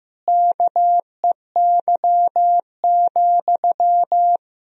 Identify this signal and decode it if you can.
Morse code, heard as a keyed tone, about 15 wpm, 700 Hz. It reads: KEY,
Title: Sentences from Top 1000 Words - Review Part 4 - 15wpm